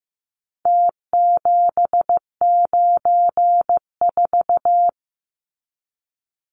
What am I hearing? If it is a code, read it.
Morse code, heard as a keyed tone, about 15 wpm, 700 Hz. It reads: T794
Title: Numbers - Contest Style - 4-Digit 15wpm